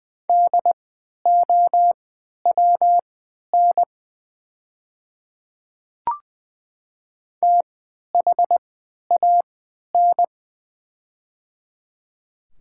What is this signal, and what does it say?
Morse code, heard as a keyed tone, about 20 wpm, spaced as for 11 wpm, 700 Hz. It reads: DOWN THAN
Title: 3x Character Spacing - Top 100 Words - 20wpm